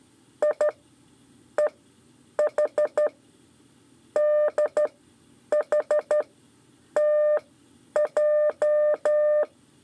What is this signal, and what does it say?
Morse code, heard as a keyed tone, about 14 wpm, spaced as for 9 wpm, 600 Hz. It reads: I EH DH TJ
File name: 5 LEZIONE RIPASSO 2